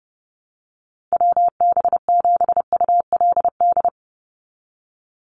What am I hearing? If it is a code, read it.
Morse code, heard as a keyed tone, about 30 wpm, 700 Hz. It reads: WB7ULD
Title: NA SOTA Chaser Callsigns 30wpm